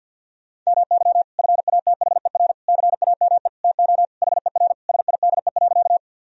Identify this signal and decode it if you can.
Morse code, heard as a keyed tone, about 52 wpm, 700 Hz. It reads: MY FATHER CAME TO HER SIDE.
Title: Sentences from Top 1000 Words - Review Part 1 - 50wpm